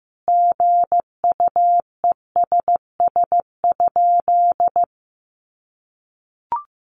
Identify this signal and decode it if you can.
Morse code, heard as a keyed tone, about 15 wpm, 700 Hz. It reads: GUESS?
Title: Sentences from Top 1500 Words 15wpm